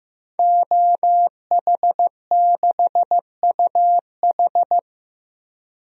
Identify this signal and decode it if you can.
Morse code, heard as a keyed tone, about 15 wpm, 700 Hz. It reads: OH6UH
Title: DXCC-100 Call Signs - 5 Characters - Repeated 2x - 15wpm